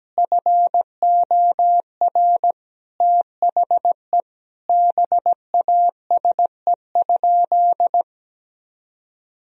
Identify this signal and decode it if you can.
Morse code, heard as a keyed tone, about 17 wpm, 700 Hz. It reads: FOR THE BASE?
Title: Sentences from Top 1000 Words - Review Part 1 - 17wpm